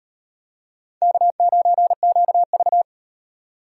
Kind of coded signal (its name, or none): Morse code